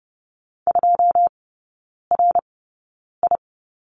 Morse code, heard as a keyed tone, about 30 wpm, spaced as for 9 wpm, 700 Hz. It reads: 2RI
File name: 081 Lesson 14e - Number 2 - Three Characters 30wpm